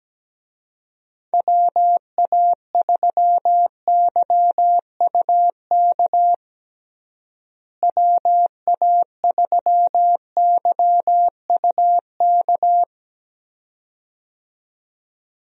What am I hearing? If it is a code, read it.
Morse code, heard as a keyed tone, about 17 wpm, 700 Hz. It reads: WA3YUK WA3YUK
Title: USA Call Signs - 6 Characters - Repeated 2x - 17wpm